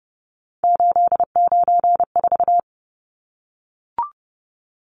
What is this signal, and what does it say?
Morse code, heard as a keyed tone, about 30 wpm, 700 Hz. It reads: 894
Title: Numbers - 3-Digit 30wpm